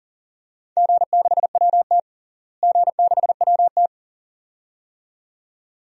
Morse code, heard as a keyed tone, about 40 wpm, 700 Hz. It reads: G6WT G6WT
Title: DXCC-100 Call Signs - 4 Characters - Repeated 2x - 40wpm